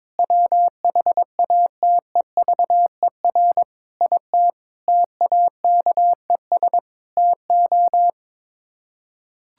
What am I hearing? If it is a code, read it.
Morse code, heard as a keyed tone, about 22 wpm, 700 Hz. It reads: WHATEVER IT TAKES TO
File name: practice / 22wpm / Sets of 6 Words - 22wpm